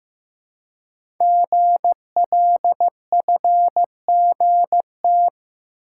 Morse code, heard as a keyed tone, about 15 wpm, 700 Hz. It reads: GLFGT
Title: Code Groups - 5 Letters - 15wpm